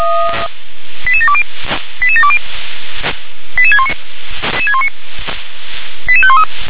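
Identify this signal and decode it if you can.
Five-tone selective call (ZVEI), heard as quick runs of five tones, tones of about 70 ms. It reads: 88720 88720 88619 88513
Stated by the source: ZVEI five-tone call 88756 received by SDR.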